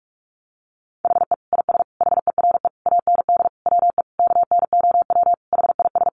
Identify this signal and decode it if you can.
Morse code, heard as a keyed tone, about 44 wpm, 700 Hz. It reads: HE IS HERE AND WE KNOW HIS
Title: Sentences from Top 800 Words 45wpm